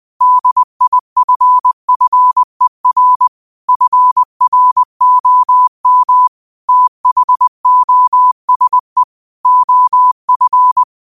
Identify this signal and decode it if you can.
Morse code, heard as a keyed tone, about 20 wpm, 1000 Hz. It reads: DIFFER FROM THOSE OF